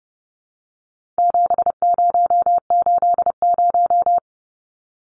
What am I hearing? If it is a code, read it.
Morse code, heard as a keyed tone, about 30 wpm, 700 Hz. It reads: 7080
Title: Numbers - 4-Digit 30wpm